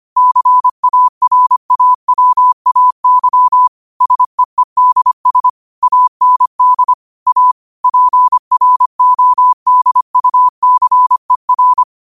Morse code, heard as a keyed tone, about 25 wpm, 1000 Hz. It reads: CARAWAY SEEDS AND A PRODUCER
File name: qotdfiles / 25wpm